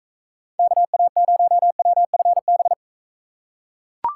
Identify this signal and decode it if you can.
Morse code, heard as a keyed tone, about 42 wpm, 700 Hz. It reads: KA0WFB